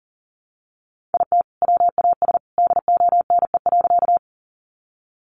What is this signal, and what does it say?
Morse code, heard as a keyed tone, about 40 wpm, 700 Hz. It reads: IT WAS DONE.
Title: Sentences from Top 500 Words 40wpm